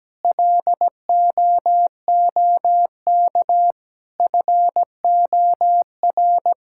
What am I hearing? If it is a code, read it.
Morse code, heard as a keyed tone, about 17 wpm, 700 Hz. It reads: LOOK FOR
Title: Sentences from Top 1600 Words 17wpm